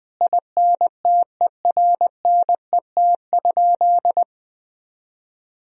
Morse code, heard as a keyed tone, about 20 wpm, 700 Hz. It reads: INTERNET?
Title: Sentences from Top 1000 Words - Review Part 4 - 20wpm